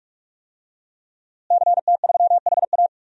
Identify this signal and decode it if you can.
Morse code, heard as a keyed tone, about 44 wpm, 700 Hz. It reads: CT3HA
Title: DXCC-100 Call Signs - 5 Characters - Repeated 2x - 45wpm